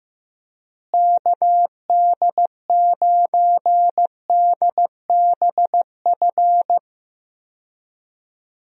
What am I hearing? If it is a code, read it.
Morse code, heard as a keyed tone, about 15 wpm, 700 Hz. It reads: KD9DBF
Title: USA Call Signs - 6 Characters - Repeated 2x - 15wpm